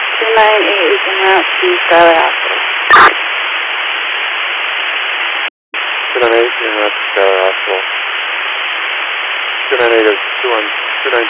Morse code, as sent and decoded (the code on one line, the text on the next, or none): none